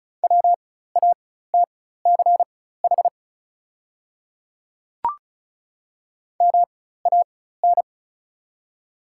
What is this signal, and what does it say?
Morse code, heard as a keyed tone, about 36 wpm, spaced as for 16 wpm, 690 Hz. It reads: WATCH MAN
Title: 4x Character Spacing - Top 300 Words - 35wpm